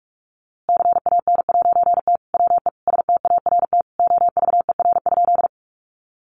Morse code, heard as a keyed tone, about 45 wpm, 700 Hz. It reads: CAN'T WE START OVER?
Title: A Sound Of Thunder 45wpm